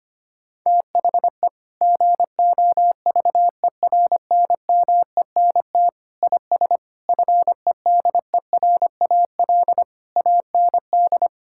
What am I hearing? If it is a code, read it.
Morse code, heard as a keyed tone, about 25 wpm, 700 Hz. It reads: THE GOVERNMENT IS FEDERAL AND
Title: Sentences from Top 1000 Words - Review Part 5 - 25wpm